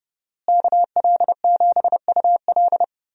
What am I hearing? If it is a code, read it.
Morse code, heard as a keyed tone, about 30 wpm, 700 Hz. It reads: KL7UL
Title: USA Call Signs - 5 Characters - Repeated 3x - 30wpm